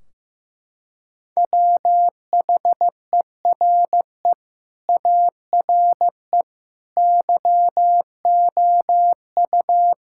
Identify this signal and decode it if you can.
Morse code, heard as a keyed tone, about 15 wpm, 700 Hz. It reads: WHERE ARE YOU
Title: Sentences from Top 1600 Words 15wpm